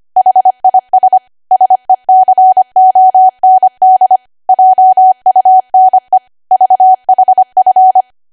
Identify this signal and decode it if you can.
Morse code, heard as a keyed tone, about 25 wpm, 750 Hz. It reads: HIS SECOND JUNE VHF